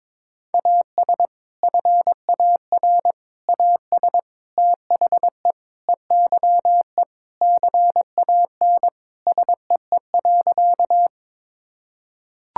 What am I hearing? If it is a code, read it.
Morse code, heard as a keyed tone, about 22 wpm, 700 Hz. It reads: AS FAR AS THE EYE CAN SEE.
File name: practice / 22wpm / Sentences with 4-Word Phrases - 22wpm